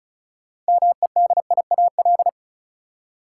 Morse code, heard as a keyed tone, about 34 wpm, 700 Hz. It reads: MEDIAL
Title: Practice QSO - Exchange 3 - More Info - 35wpm